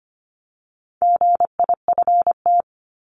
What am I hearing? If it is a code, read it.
Morse code, heard as a keyed tone, about 25 wpm, 700 Hz. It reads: GIFT
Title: Top 751-1000 Words 25wpm